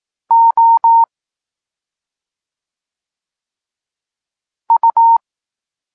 Morse code, heard as a keyed tone, about 18 wpm, spaced as for 2 wpm, 920 Hz. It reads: OU
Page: Tekst 8 – doskonalenie odbioru poznanych liter